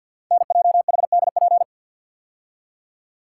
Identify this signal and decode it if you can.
Morse code, heard as a keyed tone, about 46 wpm, 700 Hz. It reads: NJHDP
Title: Code Groups - 5 Letters - 50wpm